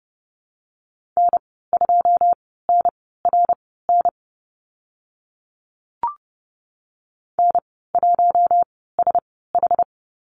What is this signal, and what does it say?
Morse code, heard as a keyed tone, about 30 wpm, spaced as for 17 wpm, 700 Hz. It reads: N2NRN N1SH
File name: Lesson 14g - Number 2 - Call Signs 30wpm